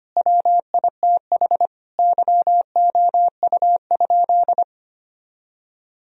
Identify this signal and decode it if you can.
Morse code, heard as a keyed tone, about 25 wpm, 700 Hz. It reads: WITH YOU?
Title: Sentences from Top 1000 Words - Review Part 1 - 25wpm